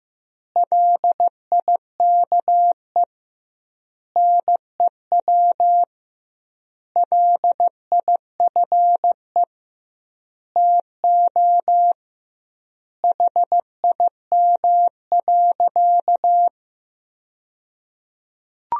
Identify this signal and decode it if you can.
Morse code, heard as a keyed tone, about 16 wpm, 700 Hz. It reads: LIKE NEW LIFE TO HIM.
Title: 2x Word-Spacing - Sentences from Top 300 Words - 15wpm